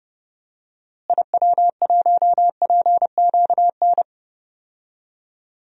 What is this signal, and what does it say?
Morse code, heard as a keyed tone, about 30 wpm, 700 Hz. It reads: IW1PQN
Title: DXCC-100 Call Signs - 6 Characters - Repeated 2x - 30wpm